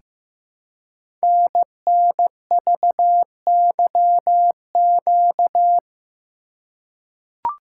Morse code, heard as a keyed tone, about 15 wpm, 700 Hz. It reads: NNVYQ
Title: Code Groups - 5 Letters - 15wpm